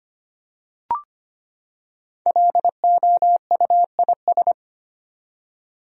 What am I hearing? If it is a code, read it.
Morse code, heard as a keyed tone, about 25 wpm, 700 Hz. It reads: LOUIS
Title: USA Common Names - 25wpm